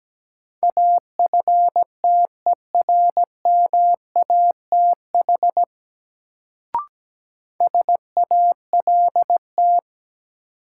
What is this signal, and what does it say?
Morse code, heard as a keyed tone, about 17 wpm, 700 Hz. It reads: AFTERMATH SALT
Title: Sets of 5 Words - New Word Drill - 17wpm